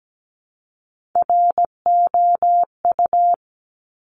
Morse code, heard as a keyed tone, about 17 wpm, 700 Hz. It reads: ROU
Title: Top Three Letter Combinations in Words 17wpm